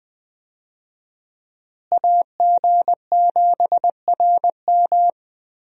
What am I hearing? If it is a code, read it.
Morse code, heard as a keyed tone, about 20 wpm, 700 Hz. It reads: AG7RM
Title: USA Call Signs - 5 Characters - Repeated 3x - 20wpm